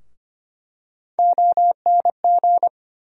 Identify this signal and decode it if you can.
Morse code, heard as a keyed tone, about 25 wpm, 700 Hz. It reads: ONG
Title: USA Call Sign Postfixes - 3 Letters 25wpm